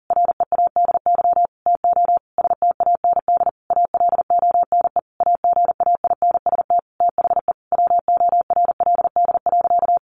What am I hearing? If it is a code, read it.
Morse code, heard as a keyed tone, about 40 wpm, 700 Hz. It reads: READY TO STAND ALONE AGAINST THE WORLD.
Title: Sentences from Top 1400 Words 40wpm